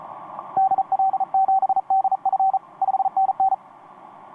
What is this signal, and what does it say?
Morse code, heard as a keyed tone, about 36 wpm, 770 Hz. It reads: DL7DF 5NN
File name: t46rrc i5fln etc 15m cw 18feb2013 1248